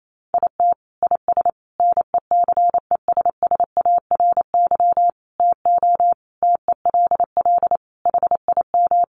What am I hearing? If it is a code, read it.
Morse code, heard as a keyed tone, about 28 wpm, 700 Hz. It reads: IT IS NECESSARY TO TELL HIM
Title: Sentences from Top 800 Words 28wpm